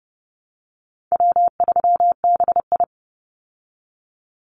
Morse code, heard as a keyed tone, about 30 wpm, 700 Hz. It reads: W3BI
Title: Easy Call Signs Repeated 2x - 30wpm